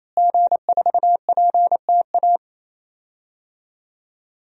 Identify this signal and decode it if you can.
Morse code, heard as a keyed tone, about 28 wpm, 700 Hz. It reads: G4PTA
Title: DXCC-100 Call Signs - 5 Characters - Repeated 3x - 28wpm